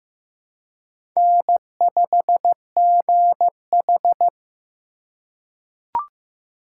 Morse code, heard as a keyed tone, about 15 wpm, 700 Hz. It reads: N5GH
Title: USA Call Signs - 4 Characters - 4x Character Spacing - 15wpm